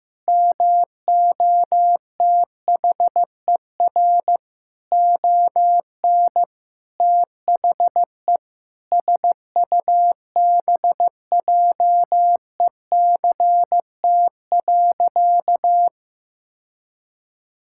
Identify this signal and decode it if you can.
Morse code, heard as a keyed tone, about 16 wpm, 700 Hz. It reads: MOTHER ON THE SUBJECT.
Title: Sentences from Top 1000 Words - Review Part 6 - 15wpm